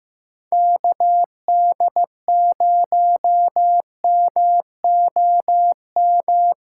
Morse code, heard as a keyed tone, about 15 wpm, 700 Hz. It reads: KD0MOM
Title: USA Callsign DE Callsign - 6 Characters - 15wpm